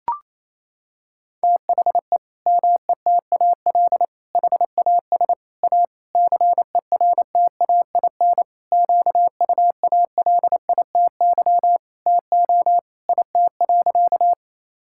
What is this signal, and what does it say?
Morse code, heard as a keyed tone, about 28 wpm, 700 Hz. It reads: THE METAL HAS A CERTAIN QUALITY TO IT.
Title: Sentences from Top 1000 Words - Review Part 6 - 28wpm